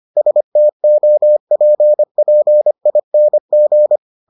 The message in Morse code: STOPPING

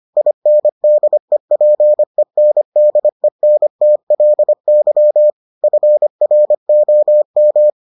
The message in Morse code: INDEPENDENTLY FROM

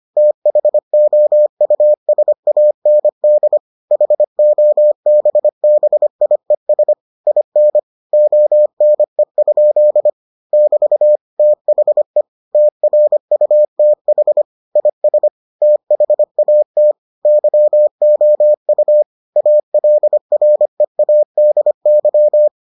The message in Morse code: THOUSAND HOBBIES IN ONE? <BT> THE TRUTH IS THAT YOU ALREADY

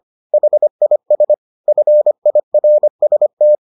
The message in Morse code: HIS FIRST

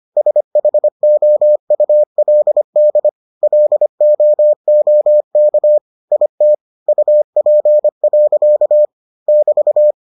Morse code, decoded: SHOULD LOOK IT UP. <BT>

